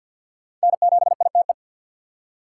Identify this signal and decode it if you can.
Morse code, heard as a keyed tone, about 49 wpm, 700 Hz. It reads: N7ITE
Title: USA Call Signs - 3x Character Spacing - 50wpm